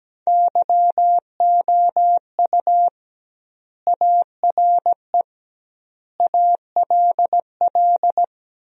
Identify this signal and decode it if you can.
Morse code, heard as a keyed tone, about 17 wpm, 700 Hz. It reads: YOU ARE ALL
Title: 2x Word-Spacing - Sentences from Top 200 Words - 17wpm